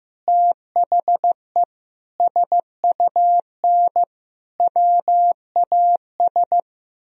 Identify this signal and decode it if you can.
Morse code, heard as a keyed tone, about 15 wpm, 700 Hz. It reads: THE SUN WAS
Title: Sentences from Top 1000 Words - Review Part 2 - 15wpm